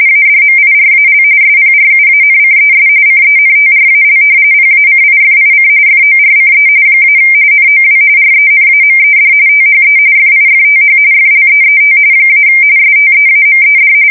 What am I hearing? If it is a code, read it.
RTTY, about 45.45 baud. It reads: ADMIRAL SPRUANCES CENTRAL PACIFIC FORCE - NOW RENAMED THE FIFTH FLEET - STEAMED
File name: fsk170 / battleshipnj